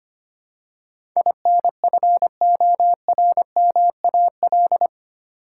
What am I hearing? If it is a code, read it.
Morse code, heard as a keyed tone, about 25 wpm, 690 Hz. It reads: INFORMAL
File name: 8 Letter Words - First 3 Letters Given - 25wpm